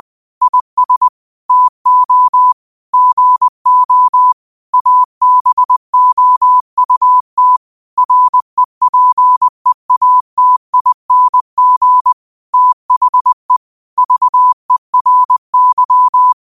Morse code, heard as a keyed tone, about 20 wpm, 1000 Hz. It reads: IS TO GO ABOUT REPEATING THE VERY